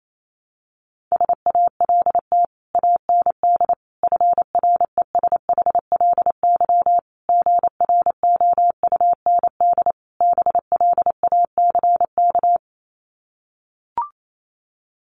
Morse code, heard as a keyed tone, about 28 wpm, 700 Hz. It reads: SALT AND FRESHLY GROUND BLACK
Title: Sets of 5 Words Encore 28wpm